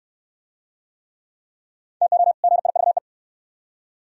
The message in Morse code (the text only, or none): TO GIVE